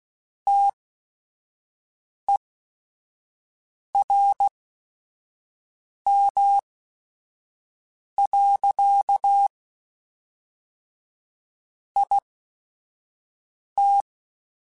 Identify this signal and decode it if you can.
Morse code, heard as a keyed tone, about 16 wpm, spaced as for 5 wpm, 800 Hz. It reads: TERM. IT
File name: Technical Subjects / Technical Subjects-mp3s / Reactance @5